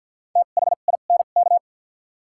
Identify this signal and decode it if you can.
Morse code, heard as a keyed tone, about 54 wpm, spaced as for 37 wpm, 700 Hz. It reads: THINK
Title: 2x Character Spacing - Top 200 Words - 50wpm